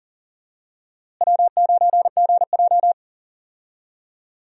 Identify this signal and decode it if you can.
Morse code, heard as a keyed tone, about 40 wpm, 700 Hz. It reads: W9GJ
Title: USA Call Signs - 4 Characters - Repeated 3x - 40wpm